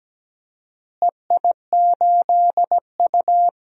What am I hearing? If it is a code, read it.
Morse code, heard as a keyed tone, about 17 wpm, 700 Hz. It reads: EI8U
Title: DXCC-100 Call Signs - 4 Characters - Repeated 2x - 17wpm